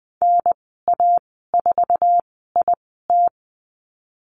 Morse code, heard as a keyed tone, about 20 wpm, spaced as for 15 wpm, 700 Hz. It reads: NA4IT
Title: 161 Lesson 26g - Letter g - Call Signs 20wpm